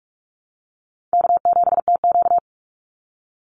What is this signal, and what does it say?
Morse code, heard as a keyed tone, about 44 wpm, 690 Hz. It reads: K7TQ
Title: NA SOTA Chaser Callsigns 45wpm